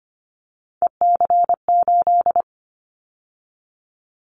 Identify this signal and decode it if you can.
Morse code, heard as a keyed tone, about 25 wpm, 700 Hz. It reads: EC8
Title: Callsign Prefixes for Top 50 DXCC - 25wpm